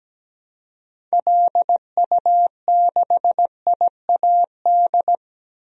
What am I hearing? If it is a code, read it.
Morse code, heard as a keyed tone, about 17 wpm, 700 Hz. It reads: LU6IAD
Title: DXCC-100 Call Signs - 6 Characters - 17wpm